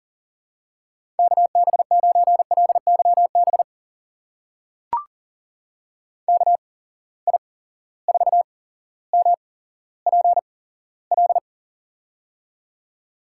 Morse code, heard as a keyed tone, about 40 wpm, spaced as for 17 wpm, 700 Hz. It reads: KB9LYB KI4MPL
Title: USA Call Signs - 6 Characters - 8x Character Spacing - 40wpm